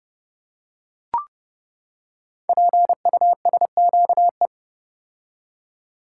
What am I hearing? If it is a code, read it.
Morse code, heard as a keyed tone, about 30 wpm, 700 Hz. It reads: PUSQE
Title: Code Groups - 5 Letters - 30wpm